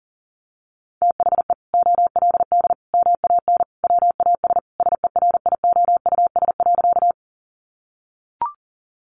Morse code, heard as a keyed tone, about 40 wpm, 700 Hz. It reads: THE OLD MAN WAS SERIOUS.
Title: Sentences from Top 1100 Words 40wpm